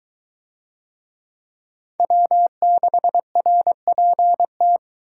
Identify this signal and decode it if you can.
Morse code, heard as a keyed tone, about 21 wpm, 700 Hz. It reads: W6RPT